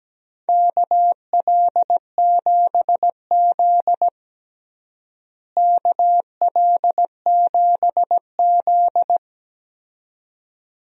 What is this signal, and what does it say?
Morse code, heard as a keyed tone, about 17 wpm, 700 Hz. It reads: KL7Z KL7Z
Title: DXCC-100 Call Signs - 4 Characters - Repeated 2x - 17wpm